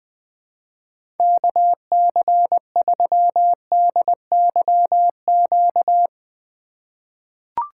Morse code, heard as a keyed tone, about 21 wpm, 700 Hz. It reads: KC3DYQ
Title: USA Call Signs - 3x Character Spacing - 20wpm